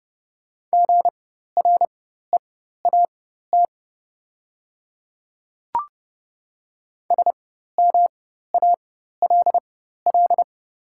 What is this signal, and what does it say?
Morse code, heard as a keyed tone, about 30 wpm, spaced as for 14 wpm, 690 Hz. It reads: GREAT SMALL
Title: 4x Character Spacing - Top 200 Words - 30wpm